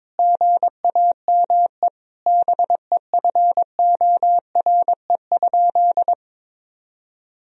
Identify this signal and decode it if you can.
Morse code, heard as a keyed tone, about 22 wpm, 700 Hz. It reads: GAME BEFORE?
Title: Sentences from Top 1000 Words - Review Part 3 - 22wpm